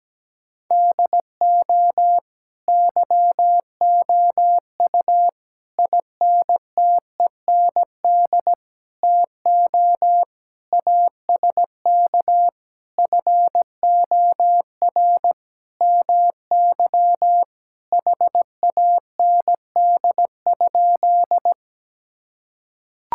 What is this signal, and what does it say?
Morse code, heard as a keyed tone, about 17 wpm, 700 Hz. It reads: DO YOU INTEND TO ASK FOR MY HAND?
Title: Sentences from Top 1600 Words 17wpm